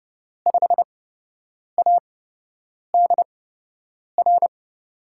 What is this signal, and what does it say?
Morse code, heard as a keyed tone, about 29 wpm, spaced as for 8 wpm, 700 Hz. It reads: 5ADR
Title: USA Call Signs - 5 Characters - 8x Character Spacing - 30wpm